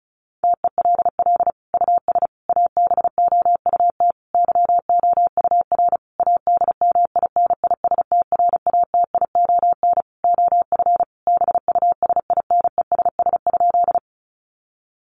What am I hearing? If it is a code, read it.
Morse code, heard as a keyed tone, about 34 wpm, 700 Hz. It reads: TELL US ABOUT YOUR ADMINISTRATION OF BUSINESS?
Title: Sentences from Top 1300 Words 35wpm